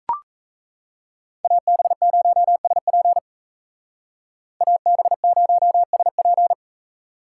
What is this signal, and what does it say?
Morse code, heard as a keyed tone, about 36 wpm, 700 Hz. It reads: AB0SP AB0SP